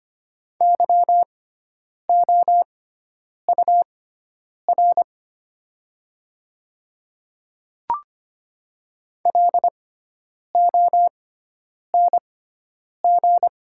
Morse code, heard as a keyed tone, about 25 wpm, spaced as for 9 wpm, 700 Hz. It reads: YOUR LONG
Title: 6x Character Spacing - Top 100 Words - 25wpm